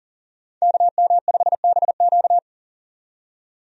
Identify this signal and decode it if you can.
Morse code, heard as a keyed tone, about 40 wpm, 700 Hz. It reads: KM5BQ
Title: USA Call Signs - 5 Characters - 40wpm